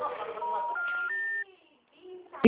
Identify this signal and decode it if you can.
Special information tone, with noise.
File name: out- -2015- - - .48450